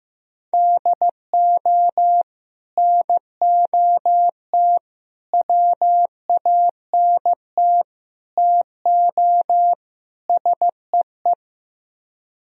Morse code, heard as a keyed tone, about 15 wpm, 700 Hz. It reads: DO NOT WANT TO SEE
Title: Speed-Racing - Sets of 5 Words 15wpm